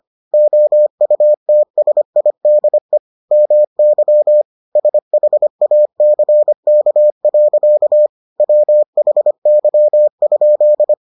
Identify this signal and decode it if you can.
Morse code, heard as a keyed tone, about 25 wpm, 600 Hz. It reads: OUTSIDE MY SHACK. WHY?